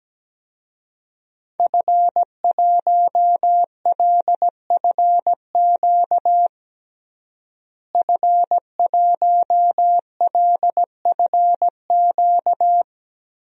Morse code, heard as a keyed tone, about 17 wpm, 700 Hz. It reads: F1LFQ F1LFQ
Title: DXCC-100 Call Signs - 5 Characters - Repeated 3x - 17wpm